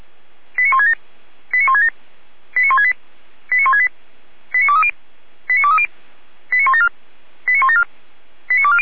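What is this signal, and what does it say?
Five-tone selective call (ZVEI), heard as quick runs of five tones, tones of about 70 ms. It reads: 89167 89167 89168 89168 89230 89230 89174 89174 89240
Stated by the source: ZVEI five-tone selective call received on an SDR.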